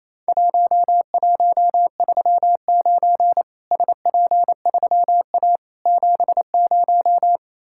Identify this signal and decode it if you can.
Morse code, heard as a keyed tone, about 28 wpm, 700 Hz. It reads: 1139 SP3A 70
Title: Worked All Europe DX Contest - QTC Exchange - 28wpm